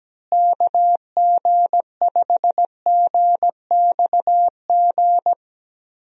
Morse code, heard as a keyed tone, about 17 wpm, 700 Hz. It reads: KG5GXG
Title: USA Call Signs - 6 Characters - Repeated 2x - 17wpm